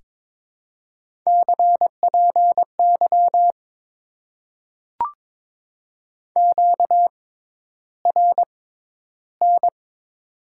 Morse code, heard as a keyed tone, about 22 wpm, spaced as for 11 wpm, 700 Hz. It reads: CPY QRN
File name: Top 90 QSO Elements - 6x Character Spacing - 22wpm